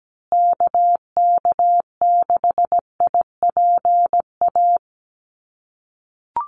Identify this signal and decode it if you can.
Morse code, heard as a keyed tone, about 17 wpm, 700 Hz. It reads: KK6IPA
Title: USA Call Signs Repeated 2x 17wpm